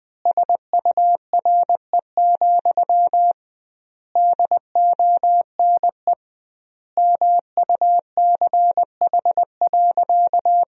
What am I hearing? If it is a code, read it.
Morse code, heard as a keyed tone, about 20 wpm, 700 Hz. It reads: SURE, DONE MUCH.
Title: 2x Word-Spacing - Sentences from Top 500 Words - 20wpm